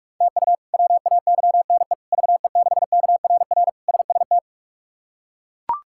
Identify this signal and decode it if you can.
Morse code, heard as a keyed tone, about 41 wpm, 700 Hz. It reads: TU WAYNE VE6KRR SST
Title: K1USN SST QSO Practice - 45wpm